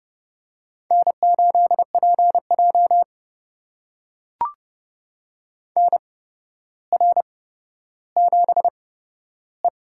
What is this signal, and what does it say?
Morse code, heard as a keyed tone, about 30 wpm, spaced as for 13 wpm, 700 Hz. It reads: N8PJ NR7E
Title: USA Call Signs - 4 Characters - 8x Character Spacing - 30wpm